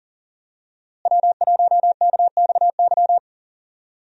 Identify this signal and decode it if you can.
Morse code, heard as a keyed tone, about 40 wpm, 700 Hz. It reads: W1KXY
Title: USA Call Signs - 5 Characters - Repeated 3x - 40wpm